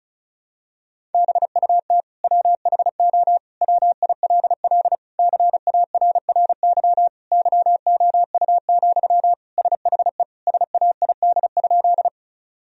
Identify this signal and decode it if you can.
Morse code, heard as a keyed tone, about 36 wpm, 700 Hz. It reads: BUT WHO WILL CARRY YOU, SHE SAID?
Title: Sentences from Top 1000 Words - Review Part 1 - 35wpm